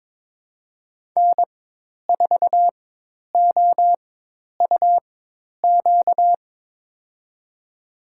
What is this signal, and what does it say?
Morse code, heard as a keyed tone, about 22 wpm, spaced as for 10 wpm, 700 Hz. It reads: N4OUQ
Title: USA Call Signs - 4x Character Spacing - 22wpm